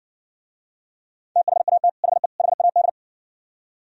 Morse code, heard as a keyed tone, about 60 wpm, 700 Hz. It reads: THAT HE HAD